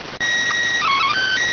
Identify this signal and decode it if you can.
SSTV (slow-scan television), PD 290.
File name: findings / S ss-portrait.gen-01 rx